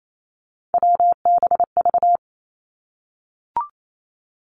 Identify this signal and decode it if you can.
Morse code, heard as a keyed tone, about 28 wpm, 700 Hz. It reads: WBV
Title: Three Challenging Letter-Number Combinations 28wpm